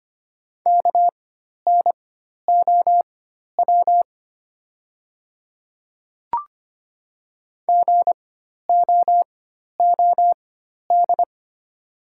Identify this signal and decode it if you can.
Morse code, heard as a keyed tone, about 25 wpm, spaced as for 12 wpm, 700 Hz. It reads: KNOW GOOD
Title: 4x Character Spacing - Top 200 Words - 25wpm